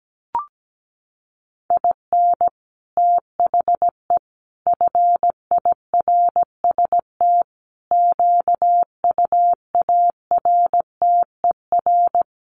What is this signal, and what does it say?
Morse code, heard as a keyed tone, about 17 wpm, 700 Hz. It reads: IN THE FIRST QUARTER
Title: Sets of 5 Words Encore2 17wpm